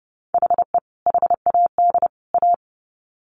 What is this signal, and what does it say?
Morse code, heard as a keyed tone, about 30 wpm, 700 Hz. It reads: HE HAD A
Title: Sets of 3 Words 30wpm